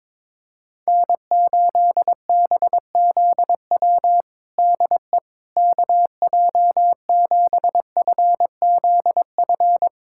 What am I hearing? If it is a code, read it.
Morse code, heard as a keyed tone, about 22 wpm, 700 Hz. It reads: N8BZW DE KJ7FZF
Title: USA Callsign DE Callsign 22wpm